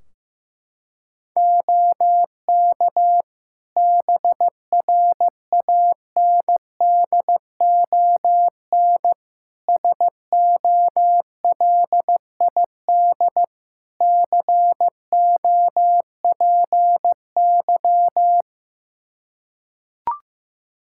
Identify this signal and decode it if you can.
Morse code, heard as a keyed tone, about 15 wpm, 700 Hz. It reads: OK BRANDON SOLID COPY
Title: K1USN SST QSO Practice 15wpm